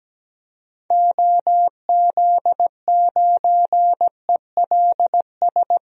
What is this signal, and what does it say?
Morse code, heard as a keyed tone, about 17 wpm, 700 Hz. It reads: OZ9ELS
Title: DXCC-100 Call Signs - 6 Characters - Repeated 2x - 17wpm